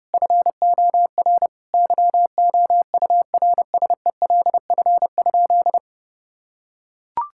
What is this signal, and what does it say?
Morse code, heard as a keyed tone, about 32 wpm, 700 Hz. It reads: FOR YOURSELF?
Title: Sentences from Top 1600 Words 30wpm